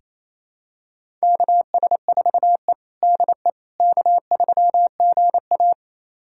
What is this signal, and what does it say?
Morse code, heard as a keyed tone, about 28 wpm, 700 Hz. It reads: KS4E DE K3GA
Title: USA Callsign DE Callsign - 4 Characters - 28wpm